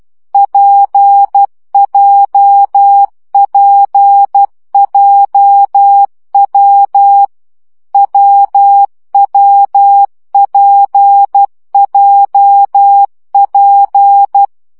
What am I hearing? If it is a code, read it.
Morse code, heard as a keyed tone, about 12 wpm, 800 Hz. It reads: PJPJW WWPJP
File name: CW gyakorlas w p j 10 baud